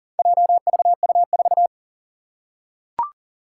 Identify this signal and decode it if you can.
Morse code, heard as a keyed tone, about 42 wpm, 700 Hz. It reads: JVU4
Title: Four Challenging Letter-Number Combinations 40wpm